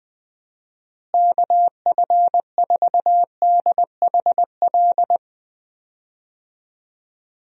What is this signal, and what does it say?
Morse code, heard as a keyed tone, about 21 wpm, 700 Hz. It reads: KF4DHL
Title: USA Call Signs - 6 Characters - Repeated 2x - 20wpm